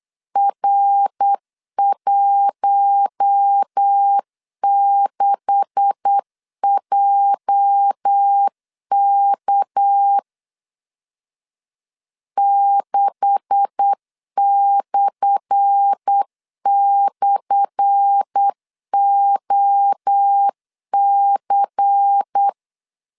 Morse code, heard as a keyed tone, about 8 wpm, 800 Hz. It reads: R16JK 6//OC
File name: LEZIONI / 10 SIGNS / 10A SLASH 8wpm 3452